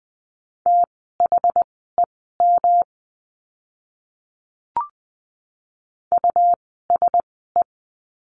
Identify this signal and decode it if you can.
Morse code, heard as a keyed tone, about 20 wpm, spaced as for 15 wpm, 700 Hz. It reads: THEM USE
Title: 115 Lesson 19f - Letter w - Words 20wpm